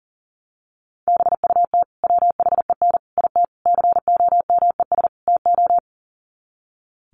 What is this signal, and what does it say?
Morse code, heard as a keyed tone, about 40 wpm, 700 Hz. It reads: BUT WHEN IT COMES TO
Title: Sets of 5 Words 40wpm